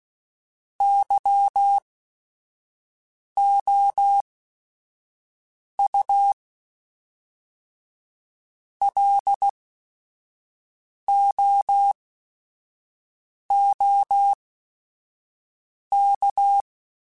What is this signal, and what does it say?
Morse code, heard as a keyed tone, about 16 wpm, spaced as for 5 wpm, 800 Hz. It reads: YOU LOOK
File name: Technical Subjects / Technical Subjects-mp3s / OR Gate @5